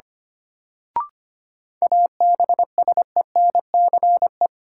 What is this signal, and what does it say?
Morse code, heard as a keyed tone, about 25 wpm, 700 Hz. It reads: ABSENCE
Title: Sets of 4 Words - New Word Drill - 25wpm